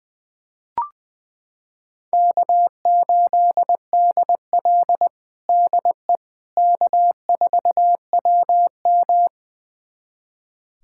Morse code, heard as a keyed tone, about 21 wpm, 700 Hz. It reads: K8DL DE K4WM
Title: USA Callsign DE Callsign - 4 Characters - 20wpm